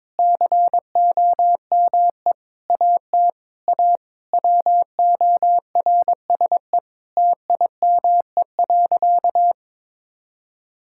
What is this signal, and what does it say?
Morse code, heard as a keyed tone, about 22 wpm, 700 Hz. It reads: COME AT A WORSE TIME.
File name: practice / 22wpm / Sentences with 8-Word Phrases - 22wpm